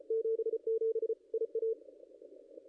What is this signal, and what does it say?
Morse code, heard as a keyed tone, about 32 wpm, 440 Hz. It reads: 77 IA
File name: contests / ss-cw-k5tr-op-n5rz -0400